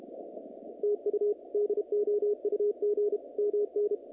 Morse code, heard as a keyed tone, about 32 wpm, 410 Hz, with radio noise. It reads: TU DOUG MN
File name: contests / naqp-cw-k5tr-op-k5pi- -2300